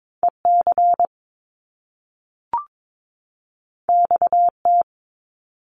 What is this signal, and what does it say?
Morse code, heard as a keyed tone, about 22 wpm, 700 Hz. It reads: EC XT
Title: Top Two Letter Combinations in Words 22wpm